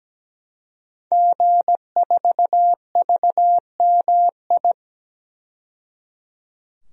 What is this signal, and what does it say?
Morse code, heard as a keyed tone, about 17 wpm, 700 Hz. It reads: G4VMI
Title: DXCC-100 Call Signs - 5 Characters - 17wpm